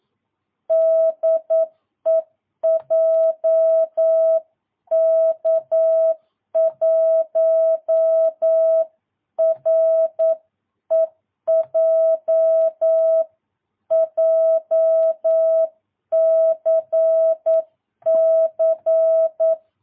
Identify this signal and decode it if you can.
Morse code, heard as a keyed tone, about 9 wpm, 650 Hz. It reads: DE JK1REJ JCC
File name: jh8cbh / 2022cw